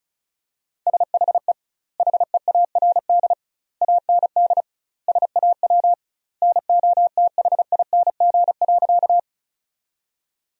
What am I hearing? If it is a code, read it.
Morse code, heard as a keyed tone, about 36 wpm, 700 Hz. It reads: SHE HEARD AND SAW NOTHING.
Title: 2x Word-Spacing - Sentences from Top 500 Words - 35wpm